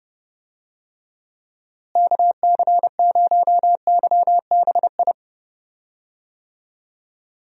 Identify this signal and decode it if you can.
Morse code, heard as a keyed tone, about 30 wpm, 700 Hz. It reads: KC0YBI
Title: USA Call Signs - 6 Characters - Repeated 2x - 30wpm